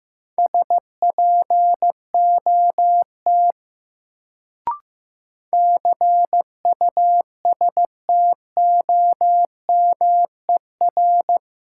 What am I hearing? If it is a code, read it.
Morse code, heard as a keyed tone, about 15 wpm, 700 Hz. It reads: SPOT CUSTOMER
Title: Sentences from Top 1100 Words - New Word Drill - 15wpm